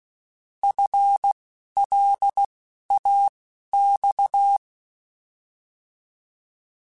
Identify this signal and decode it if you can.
Morse code, heard as a keyed tone, about 16 wpm, spaced as for 12 wpm, 800 Hz. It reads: FLAX
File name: FL / FL mp3s / Fast FL @10